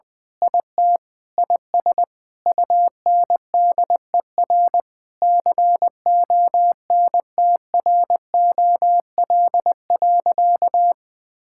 Morse code, heard as a keyed tone, about 20 wpm, 700 Hz. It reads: IT IS UNDER CONTROL.
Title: Sentences with 9-Word Phrases - Spoken Beginning - 20wpm